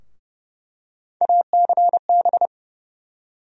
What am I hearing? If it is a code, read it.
Morse code, heard as a keyed tone, about 32 wpm, 700 Hz. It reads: ACB
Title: USA Call Sign Postfixes - 3 Letters 30wpm